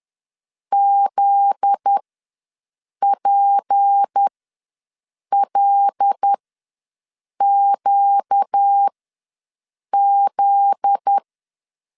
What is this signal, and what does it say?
Morse code, heard as a keyed tone, about 10 wpm, spaced as for 6 wpm, 800 Hz. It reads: ZPLQZ
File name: LEZIONI / 7 GF9Z / 7 GF9Z ws 10wpm 3919